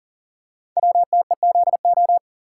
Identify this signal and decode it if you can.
Morse code, heard as a keyed tone, about 42 wpm, 700 Hz. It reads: WTEZO
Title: Code Groups - 5 Letters - 40wpm